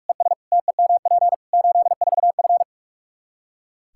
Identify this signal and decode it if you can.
Morse code, heard as a keyed tone, about 41 wpm, 700 Hz. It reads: ES TEMP 84F
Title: Practice QSO - Exchange 2 - Info - 45wpm